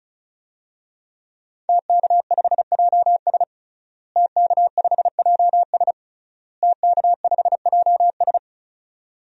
Morse code, heard as a keyed tone, about 36 wpm, 700 Hz. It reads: TK5JS TK5JS TK5JS
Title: DXCC-100 Call Signs - 5 Characters - Repeated 3x - 35wpm